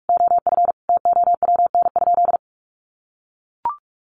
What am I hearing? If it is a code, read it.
Morse code, heard as a keyed tone, about 44 wpm, 700 Hz. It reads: OF TOWN?
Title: Sentences from Top 1300 Words 45wpm